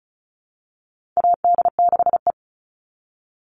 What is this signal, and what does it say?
Morse code, heard as a keyed tone, about 34 wpm, 700 Hz. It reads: AD6E
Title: Call Signs - 35wpm